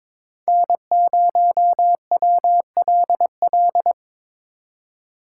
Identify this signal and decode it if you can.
Morse code, heard as a keyed tone, about 22 wpm, 700 Hz. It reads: N0WLL
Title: USA Call Signs - 3x Character Spacing - 22wpm